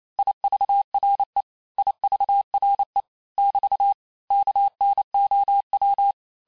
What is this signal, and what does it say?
Morse code, heard as a keyed tone, about 28 wpm, 800 Hz. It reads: IVRE IVRE <BT> KNOW
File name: cours-du-mercredi / audios / cours-865-teste-les-mots-vitesse-27mpm